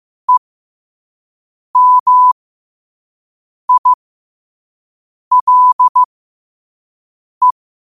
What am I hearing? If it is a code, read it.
Morse code, heard as a keyed tone, about 15 wpm, spaced as for 5 wpm, 1000 Hz. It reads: EMILE
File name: qotdfiles / 5wpm